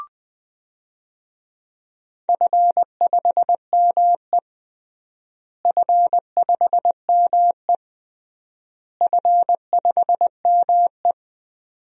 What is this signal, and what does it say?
Morse code, heard as a keyed tone, about 20 wpm, 700 Hz. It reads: F5ME F5ME F5ME
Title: DXCC-100 Call Signs - 4 Characters - Repeated 3x - 20wpm